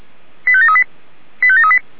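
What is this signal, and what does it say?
Five-tone selective call (ZVEI), heard as quick runs of five tones, tones of about 70 ms. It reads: 85738 85738
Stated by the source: ZVEI five-tone selective call received on SDR.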